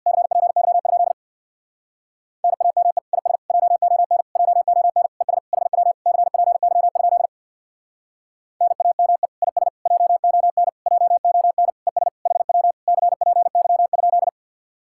Morse code, heard as a keyed tone, about 62 wpm, 700 Hz. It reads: CPY? NAME IS JON JON ES HW CPY? NAME IS JON JON ES HW CPY?
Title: Speed-Racing Common QSO Phrases Encore - 50wpm